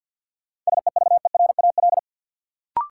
Morse code, heard as a keyed tone, about 53 wpm, 700 Hz. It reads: SEVERAL
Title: Sentences from Top 400 Words - New Word Drill - 50wpm